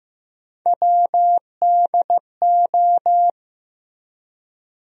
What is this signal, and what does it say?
Morse code, heard as a keyed tone, about 15 wpm, 700 Hz. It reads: WDO
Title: USA Call Sign Postfixes - 3 Letters 15wpm